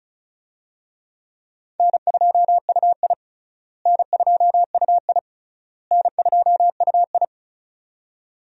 Morse code, heard as a keyed tone, about 35 wpm, 700 Hz. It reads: N2UI N2UI N2UI
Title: USA Call Signs - 4 Characters - Repeated 3x - 35wpm